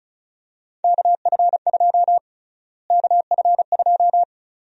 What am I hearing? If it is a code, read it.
Morse code, heard as a keyed tone, about 36 wpm, 700 Hz. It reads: KF2 KF2
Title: USA Call Sign Prefixes - Repeated 2x - 35wpm